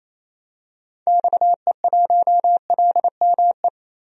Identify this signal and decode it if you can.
Morse code, heard as a keyed tone, about 28 wpm, 700 Hz. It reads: XE1LME
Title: DXCC-100 Call Signs - 6 Characters - 28wpm